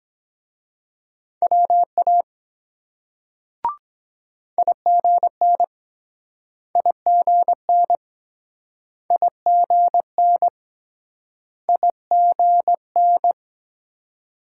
Morse code, heard as a keyed tone, about 22 wpm, 700 Hz. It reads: WA IGN IGN IGN IGN